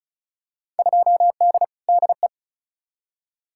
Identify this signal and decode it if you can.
Morse code, heard as a keyed tone, about 34 wpm, 700 Hz. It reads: 2D DE
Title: Field Day Exchanges Encore - 35wpm